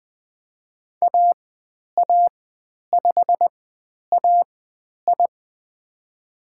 Morse code, heard as a keyed tone, about 20 wpm, spaced as for 10 wpm, 700 Hz. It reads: AA5AI
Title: 095 Lesson 16g - Letter c - Call Signs 20wpm